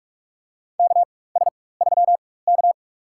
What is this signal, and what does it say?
Morse code, heard as a keyed tone, about 44 wpm, spaced as for 21 wpm, 700 Hz. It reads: KS3K
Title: USA Call Signs - 4 Characters - 4x Character Spacing - 45wpm